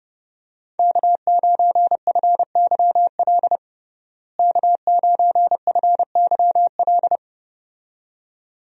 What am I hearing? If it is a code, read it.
Morse code, heard as a keyed tone, about 30 wpm, 700 Hz. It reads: K9FYL K9FYL
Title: USA Call Signs - 5 Characters - Repeated 2x - 30wpm